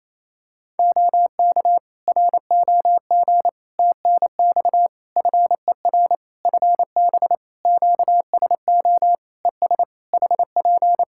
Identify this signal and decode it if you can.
Morse code, heard as a keyed tone, about 28 wpm, 700 Hz. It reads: OK ROG TNX FER FB QSO ES HP
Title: Practice QSO - Exchange 4 - Ending - 28wpm